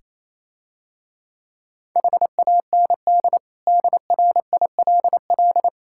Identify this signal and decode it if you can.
Morse code, heard as a keyed tone, about 29 wpm, 700 Hz. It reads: HAND DRILL